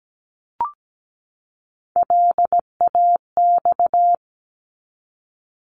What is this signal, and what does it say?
Morse code, heard as a keyed tone, about 17 wpm, 700 Hz. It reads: LAX
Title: ARRL Sections 17wpm